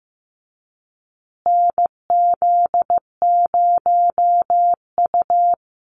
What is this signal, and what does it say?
Morse code, heard as a keyed tone, about 14 wpm, 700 Hz. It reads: NZ0U
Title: USA Call Signs Repeated 3x 15wpm